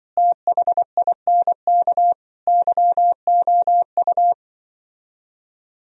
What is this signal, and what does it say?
Morse code, heard as a keyed tone, about 25 wpm, 700 Hz. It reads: THINK YOU
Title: Speed-Racing - Sets of 4 Words 20wpm